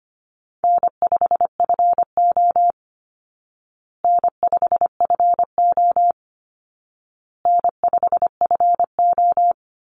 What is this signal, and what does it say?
Morse code, heard as a keyed tone, about 25 wpm, 700 Hz. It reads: N5FO N5FO N5FO
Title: Call Signs Repeated 2x - 25wpm